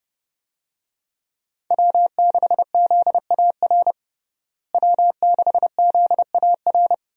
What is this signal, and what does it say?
Morse code, heard as a keyed tone, about 32 wpm, 700 Hz. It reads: W6ZAR W6ZAR
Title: USA Call Signs - 5 Characters - Repeated 2x - 30wpm